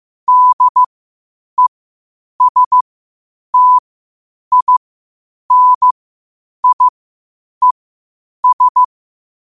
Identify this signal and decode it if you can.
Morse code, heard as a keyed tone, about 15 wpm, spaced as for 8 wpm, 1000 Hz. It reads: DESTINIES
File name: qotdfiles / 8wpm